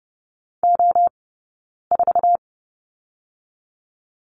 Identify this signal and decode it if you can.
Morse code, heard as a keyed tone, about 30 wpm, 700 Hz. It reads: O 4
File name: 092 Lesson 16d - Letter c - Two Characters 30wpm